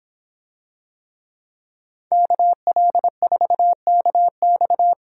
Morse code, heard as a keyed tone, about 26 wpm, 700 Hz. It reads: KL4KX